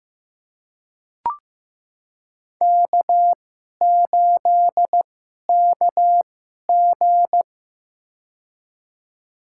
Morse code, heard as a keyed tone, about 15 wpm, spaced as for 11 wpm, 700 Hz. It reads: K8KG
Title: USA Call Signs - 4 Characters - 2x Character Spacing - 15wpm